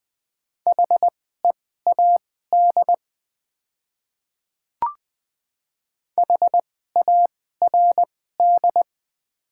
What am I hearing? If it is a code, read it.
Morse code, heard as a keyed tone, about 20 wpm, spaced as for 14 wpm, 700 Hz. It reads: HEAD HARD
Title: 2x Character Spacing - Top 300 Words - 20wpm